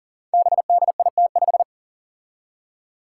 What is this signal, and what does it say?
Morse code, heard as a keyed tone, about 38 wpm, 700 Hz. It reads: BDIT5
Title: Five Challenging Letter-Number Combinations 40wpm